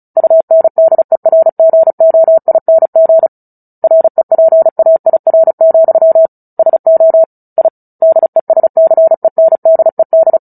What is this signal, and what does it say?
Morse code, heard as a keyed tone, about 35 wpm, 650 Hz. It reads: UNDERGOING REPAIR, SO I DESCENDED
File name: Book Two / 09-Chapter 9 - 35 WPM